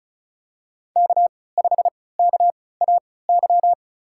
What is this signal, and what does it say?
Morse code, heard as a keyed tone, about 36 wpm, spaced as for 20 wpm, 700 Hz. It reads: K5KAY
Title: USA Call Signs - 5 Characters - 3x Character Spacing - 35wpm